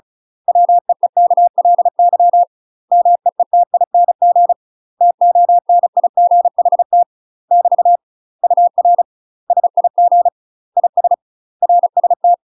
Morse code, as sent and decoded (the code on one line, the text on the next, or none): .-- . -.- .-.. -.-- / -- . - .. -. --. / - --- -. .. --. .... - / -...- / ..- .-. / ... .. --. / .. ... / .-. ... -
WEEKLY MEETING TONIGHT <BT> UR SIG IS RST